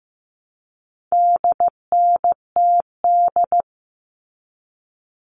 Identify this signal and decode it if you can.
Morse code, heard as a keyed tone, about 15 wpm, 700 Hz. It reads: DNTD
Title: All Cut Numbers - 4-Digit 15wpm